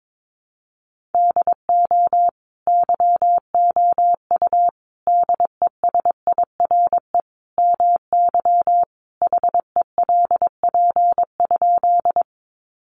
Morse code, heard as a keyed tone, about 22 wpm, 700 Hz. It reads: DO YOU DESIRE MY HELP?
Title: Sentences from Top 1100 Words 22wpm